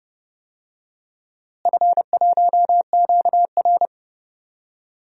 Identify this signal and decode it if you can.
Morse code, heard as a keyed tone, about 30 wpm, 700 Hz. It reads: F1QR
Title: DXCC-100 Call Signs - 4 Characters - Repeated 3x - 30wpm